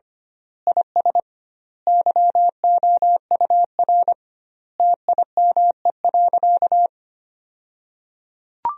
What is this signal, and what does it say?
Morse code, heard as a keyed tone, about 25 wpm, 700 Hz. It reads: IS YOUR TIME.
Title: 2x Word-Spacing - Sentences from Top 100 Words - 25wpm